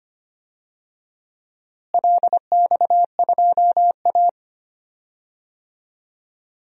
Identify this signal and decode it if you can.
Morse code, heard as a keyed tone, about 25 wpm, 700 Hz. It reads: LX2A
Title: DXCC-100 Call Signs - 4 Characters - Repeated 2x - 25wpm